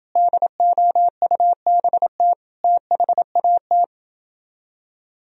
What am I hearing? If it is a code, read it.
Morse code, heard as a keyed tone, about 29 wpm, 700 Hz. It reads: DOUBT THAT